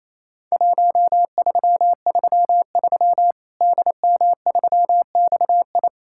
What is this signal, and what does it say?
Morse code, heard as a keyed tone, about 28 wpm, 700 Hz. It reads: 1333 DM3XI
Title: Worked All Europe DX Contest - QTC Exchange - 28wpm